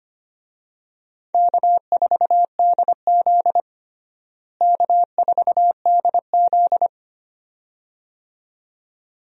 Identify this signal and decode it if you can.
Morse code, heard as a keyed tone, about 25 wpm, 700 Hz. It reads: K4DZ K4DZ
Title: USA Call Signs - 4 Characters - Repeated 2x - 25wpm